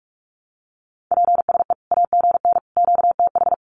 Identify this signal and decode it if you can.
Morse code, heard as a keyed tone, about 44 wpm, 700 Hz. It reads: PSE AGN QTH
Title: Common QSO Phrases 45wpm